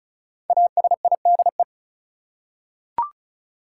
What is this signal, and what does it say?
Morse code, heard as a keyed tone, about 36 wpm, 700 Hz. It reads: ASIDE
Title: Five Letter Words - 35wpm